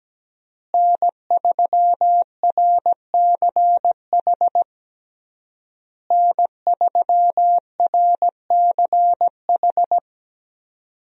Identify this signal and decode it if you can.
Morse code, heard as a keyed tone, about 17 wpm, 700 Hz. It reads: N3RCH N3RCH
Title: USA Call Signs - 5 Characters - Repeated 3x - 17wpm